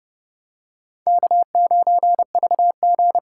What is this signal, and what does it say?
Morse code, heard as a keyed tone, about 29 wpm, 700 Hz. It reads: K9VG